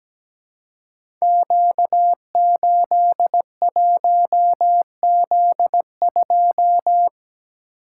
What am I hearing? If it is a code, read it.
Morse code, heard as a keyed tone, about 17 wpm, 700 Hz. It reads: Q81Z2
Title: Code Groups - 5 Letter:Numbers - 17wpm